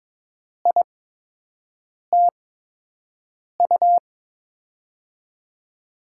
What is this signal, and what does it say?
Morse code, heard as a keyed tone, about 21 wpm, spaced as for 6 wpm, 700 Hz. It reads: ITU
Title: USA Call Signs - 5 Characters - 8x Character Spacing - 22wpm